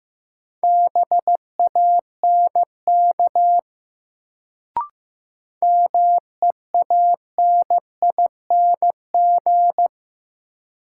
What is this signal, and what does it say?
Morse code, heard as a keyed tone, about 15 wpm, 700 Hz. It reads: BANK MEANING
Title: Sentences from Top 900 Words - New Word Drill - 15wpm